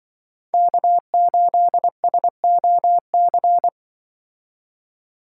Morse code, heard as a keyed tone, about 24 wpm, 700 Hz. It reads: K8SOC
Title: Speed Racing - USA Call Signs 20wpm